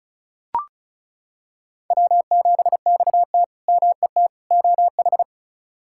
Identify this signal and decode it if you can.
Morse code, heard as a keyed tone, about 36 wpm, 700 Hz. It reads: W7XT MET OH
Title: K1USN SST QSO Practice - 35wpm